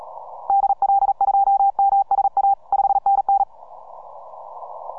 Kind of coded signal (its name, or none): Morse code